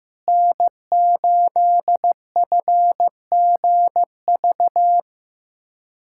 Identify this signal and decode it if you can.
Morse code, heard as a keyed tone, about 16 wpm, 700 Hz. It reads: N8FGV
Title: USA Callsign DE Callsign - 5 Characters - 15wpm